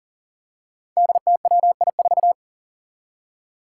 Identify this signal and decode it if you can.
Morse code, heard as a keyed tone, about 40 wpm, 700 Hz. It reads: DTWI4
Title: Five Challenging Letter-Number Combinations 40wpm